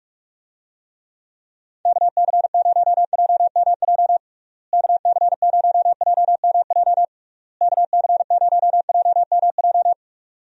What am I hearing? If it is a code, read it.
Morse code, heard as a keyed tone, about 45 wpm, 700 Hz. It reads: KC0JMJ KC0JMJ KC0JMJ
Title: USA Call Signs - 6 Characters - Repeated 3x - 45wpm